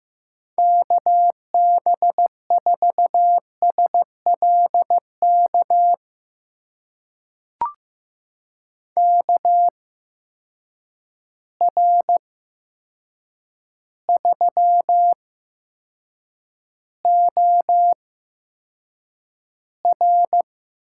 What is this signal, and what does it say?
Morse code, heard as a keyed tone, about 15 wpm, spaced as for 7 wpm, 700 Hz. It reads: KB4SLK KR3OR
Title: USA Call Signs - 6 Characters - 8x Character Spacing - 15wpm